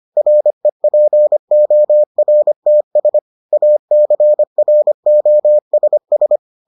Morse code, heard as a keyed tone, about 25 wpm, 600 Hz. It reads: REPORTS ACROSS